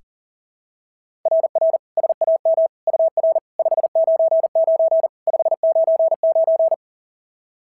Morse code, heard as a keyed tone, about 40 wpm, 690 Hz. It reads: RR SAM UR 599 599
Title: SOTA QSO Practice 40wpm